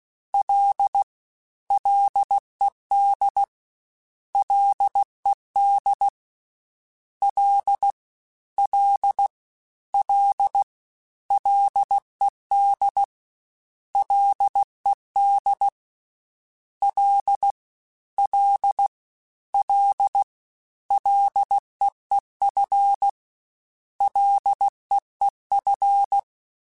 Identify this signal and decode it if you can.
Morse code, heard as a keyed tone, about 16 wpm, 800 Hz. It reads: L LED LED L L L LED LED L L L LEEF LEEF
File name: Prefixes 3x2 Twice @15